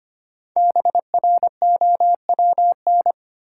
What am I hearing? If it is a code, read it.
Morse code, heard as a keyed tone, about 25 wpm, 700 Hz. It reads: BROWN
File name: 5 Letter Words - First 3 Letters Given - 25wpm